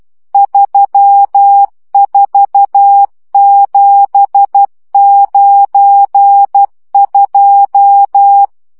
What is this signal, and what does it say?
Morse code, heard as a keyed tone, about 12 wpm, 800 Hz. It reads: 34792
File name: morze vizsga 2011-3 10